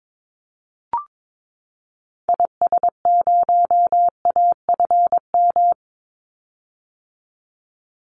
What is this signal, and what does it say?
Morse code, heard as a keyed tone, about 22 wpm, 700 Hz. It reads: IS0AFM
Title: Call Signs - 22wpm